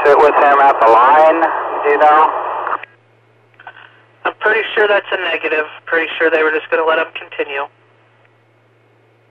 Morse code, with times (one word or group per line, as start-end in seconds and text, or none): none